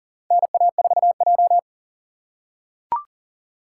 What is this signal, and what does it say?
Morse code, heard as a keyed tone, about 42 wpm, 700 Hz. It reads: NA4J
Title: USA Call Signs - 4 Characters - 4x Character Spacing - 40wpm